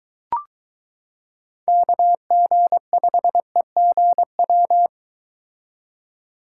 Morse code, heard as a keyed tone, about 23 wpm, 700 Hz. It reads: KG5EGW